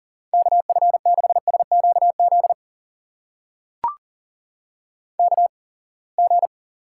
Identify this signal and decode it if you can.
Morse code, heard as a keyed tone, about 37 wpm, spaced as for 27 wpm, 700 Hz. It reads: KF6SQZ KG
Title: USA Call Signs - 6 Characters - 8x Character Spacing - 40wpm